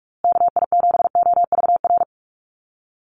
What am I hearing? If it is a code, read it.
Morse code, heard as a keyed tone, about 44 wpm, 700 Hz. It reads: KI7OVR
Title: USA Call Signs Repeated 3x 45wpm